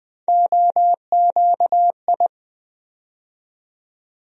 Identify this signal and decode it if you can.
Morse code, heard as a keyed tone, about 20 wpm, 700 Hz. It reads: OQI
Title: USA Call Sign Postfixes - 3 Letters 20wpm